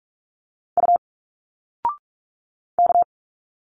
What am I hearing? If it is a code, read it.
Morse code, heard as a keyed tone, about 45 wpm, 700 Hz. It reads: U K
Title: Single Letters 45wpm